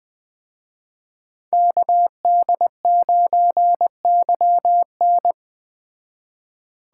Morse code, heard as a keyed tone, about 20 wpm, 700 Hz. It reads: KD9YN
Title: Speed Racing - USA Call Signs 17wpm